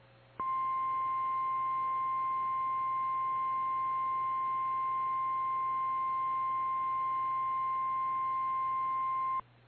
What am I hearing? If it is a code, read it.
Emergency Alert System (SAME) tones (the 1050 Hz attention tone). It is heard from 0.4 to 9.4 s.
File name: eas / KKPRKXPN / KKPRKXPN